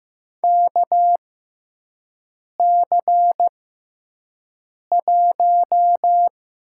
Morse code, heard as a keyed tone, about 15 wpm, spaced as for 5 wpm, 700 Hz. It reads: KC1
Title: USA Call Signs - 6x Character Spacing - 15wpm